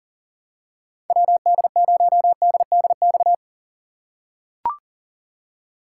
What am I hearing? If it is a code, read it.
Morse code, heard as a keyed tone, about 40 wpm, 700 Hz. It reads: WD0DDX
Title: USA Call Signs - 3x Character Spacing - 40wpm